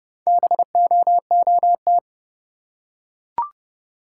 Morse code, heard as a keyed tone, about 30 wpm, 700 Hz. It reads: BOOT
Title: Four Letter Words - 30wpm